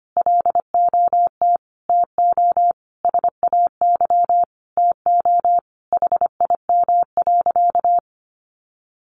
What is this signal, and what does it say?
Morse code, heard as a keyed tone, about 25 wpm, 700 Hz. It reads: LOT TO SAY TO HIM.
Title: Sentences from Top 1000 Words 25wpm